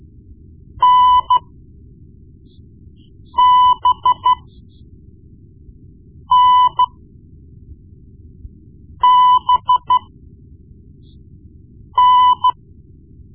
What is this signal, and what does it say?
Morse code, heard as a keyed tone, about 14 wpm, spaced as for 4 wpm, 1000 Hz. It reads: NBNBN
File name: Morse Lessons / Morse Lessons / Tape-1B